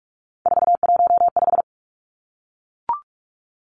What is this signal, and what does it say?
Morse code, heard as a keyed tone, about 44 wpm, 700 Hz. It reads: 415
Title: Numbers - 3-Digit 45wpm